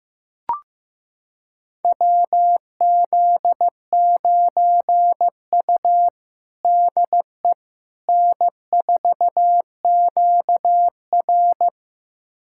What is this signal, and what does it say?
Morse code, heard as a keyed tone, about 15 wpm, 700 Hz. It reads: WZ9U DE N4QR
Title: USA Callsign DE Callsign - 4 Characters - 15wpm